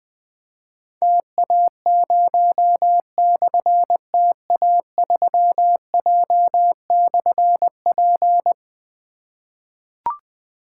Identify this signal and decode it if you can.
Morse code, heard as a keyed tone, about 20 wpm, 700 Hz. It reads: TA0/TA3J/P
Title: Difficult Call Signs 20wpm